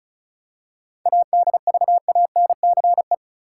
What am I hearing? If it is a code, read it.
Morse code, heard as a keyed tone, about 36 wpm, 700 Hz. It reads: ADVANCE
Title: Seven Letter Words - 35wpm